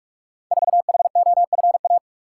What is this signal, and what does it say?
Morse code, heard as a keyed tone, about 44 wpm, 700 Hz. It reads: 4HOFA
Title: Code Groups - 5 Letter:Numbers - 45wpm